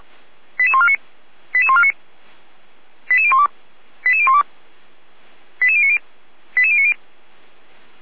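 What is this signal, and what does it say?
Five-tone selective call (ZVEI), heard as quick runs of five tones, tones of about 70 ms. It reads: 88149 88149 88012 88012 88090 88090
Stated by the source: ZVEI five-tone call 88150 received by SDR.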